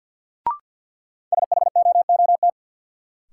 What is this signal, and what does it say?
Morse code, heard as a keyed tone, about 50 wpm, 700 Hz. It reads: SHOOT
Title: Sentences from Top 1200 Words - New Word Drill Alt - 50wpm